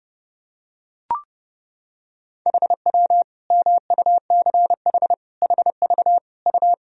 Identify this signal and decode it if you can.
Morse code, heard as a keyed tone, about 32 wpm, 700 Hz. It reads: HW MUCH HV U
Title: Sentences from Top 300 Words and Abbreviations 30wpm